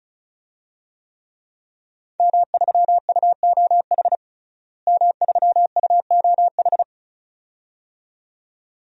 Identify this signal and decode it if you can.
Morse code, heard as a keyed tone, about 36 wpm, 700 Hz. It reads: M3UOH M3UOH
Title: DXCC-100 Call Signs - 5 Characters - Repeated 2x - 35wpm